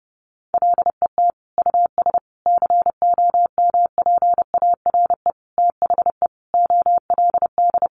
Morse code, heard as a keyed tone, about 28 wpm, 700 Hz. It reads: LET US COMPARE THE OLD
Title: Sentences from Top 1300 Words 30wpm